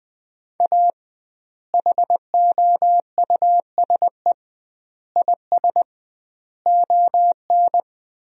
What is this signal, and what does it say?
Morse code, heard as a keyed tone, about 21 wpm, 700 Hz. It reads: A HOUSE IS ON
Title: 2x Word-Spacing - Sentences from Top 400 Words - 20wpm